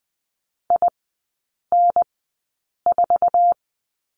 Morse code, heard as a keyed tone, about 20 wpm, spaced as for 8 wpm, 700 Hz. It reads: IN4
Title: 047 Lesson 9e - Number 4 - Three Characters 20wpm